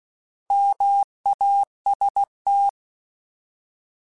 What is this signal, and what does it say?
Morse code, heard as a keyed tone, about 16 wpm, 800 Hz. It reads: MAST
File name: Alphabetical / Alphabetical mp3s / Ms @15